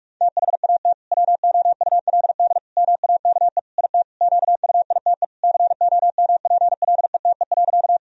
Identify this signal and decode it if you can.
Morse code, heard as a keyed tone, about 46 wpm, 700 Hz. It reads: THAT WOULD MAKE IT QUITE COMPLETE.
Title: Sentences from Top 1000 Words - Review Part 2 - 45wpm